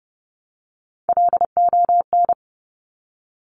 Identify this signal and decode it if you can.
Morse code, heard as a keyed tone, about 30 wpm, 700 Hz. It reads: LON
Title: Top Three Letter Combinations in Words 30wpm